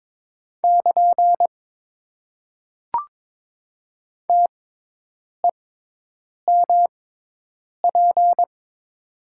Morse code, heard as a keyed tone, about 22 wpm, spaced as for 8 wpm, 700 Hz. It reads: <KN> TEMP
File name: Top 90 QSO Elements - 6x Character Spacing - 22wpm